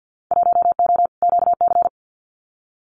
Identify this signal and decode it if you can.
Morse code, heard as a keyed tone, about 50 wpm, 700 Hz. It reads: 1O QC
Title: Winter Field Day Exchanges 50wpm